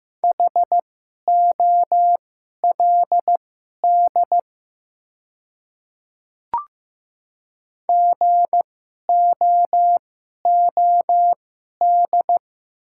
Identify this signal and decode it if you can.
Morse code, heard as a keyed tone, about 15 wpm, spaced as for 11 wpm, 700 Hz. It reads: HOLD GOOD
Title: 2x Character Spacing - Top 500 Words - 15wpm